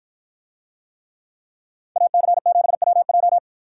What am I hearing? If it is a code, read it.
Morse code, heard as a keyed tone, about 53 wpm, 700 Hz. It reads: AC7WJ